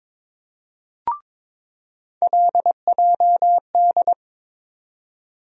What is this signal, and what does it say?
Morse code, heard as a keyed tone, about 22 wpm, 700 Hz. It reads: LJD
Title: USA Call Sign Postfixes - 3 Letters 22wpm